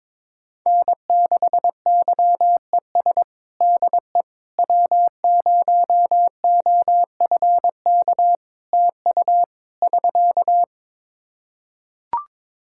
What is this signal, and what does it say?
Morse code, heard as a keyed tone, about 22 wpm, 700 Hz. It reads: N6YES DE W0OFK TU <SK>
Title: K1USN SST QSO Practice 22wpm